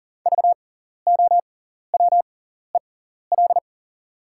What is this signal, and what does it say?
Morse code, heard as a keyed tone, about 38 wpm, spaced as for 14 wpm, 700 Hz. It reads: VOWEL
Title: 6x Character Spacing - Top 500 Words - 40wpm